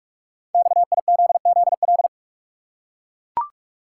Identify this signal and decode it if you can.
Morse code, heard as a keyed tone, about 44 wpm, 690 Hz. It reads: XIZZL
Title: Code Groups - 5 Letters - 45wpm